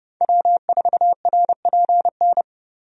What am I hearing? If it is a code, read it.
Morse code, heard as a keyed tone, about 30 wpm, 700 Hz. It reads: W4RPN
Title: USA Call Signs - 5 Characters - 30wpm